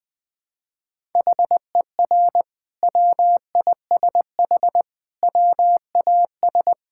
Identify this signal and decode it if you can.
Morse code, heard as a keyed tone, about 20 wpm, 700 Hz. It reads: HER WISH WAS
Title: Sentences from Top 1600 Words 20wpm